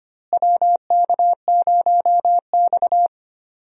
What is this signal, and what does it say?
Morse code, heard as a keyed tone, about 25 wpm, 700 Hz. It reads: WK0X
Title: USA Call Signs - 4 Characters - Repeated 2x - 25wpm